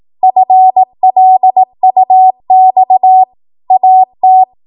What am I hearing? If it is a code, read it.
Morse code, heard as a keyed tone, about 18 wpm, 750 Hz. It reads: FLUX AT
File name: Morse / 18WPM